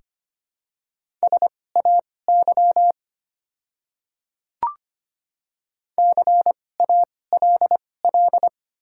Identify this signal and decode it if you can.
Morse code, heard as a keyed tone, about 25 wpm, spaced as for 18 wpm, 700 Hz. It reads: SAY CALL
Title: 2x Character Spacing - Top 200 Words - 25wpm